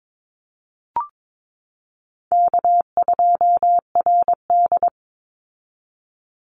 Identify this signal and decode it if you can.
Morse code, heard as a keyed tone, about 22 wpm, 700 Hz. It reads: K2RD
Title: Easy Call Signs Repeated 3x - 22wpm